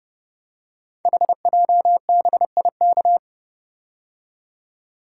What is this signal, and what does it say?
Morse code, heard as a keyed tone, about 28 wpm, 700 Hz. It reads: HJBIK
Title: Code Groups - 5 Letters - 30wpm